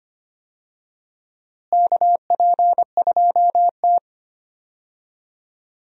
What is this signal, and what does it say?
Morse code, heard as a keyed tone, about 25 wpm, 700 Hz. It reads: KP2T
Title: USA Call Signs - 4 Characters - Repeated 3x - 25wpm